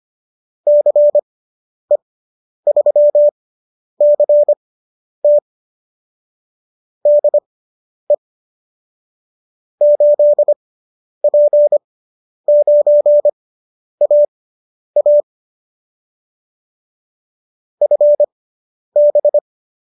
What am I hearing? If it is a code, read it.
Morse code, heard as a keyed tone, about 25 wpm, spaced as for 10 wpm, 590 Hz. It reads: CE3CT DE 8P9AA FB